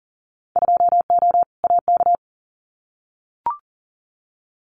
Morse code, heard as a keyed tone, about 40 wpm, 700 Hz. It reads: 2O AK
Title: Winter Field Day Exchanges Encore 40wpm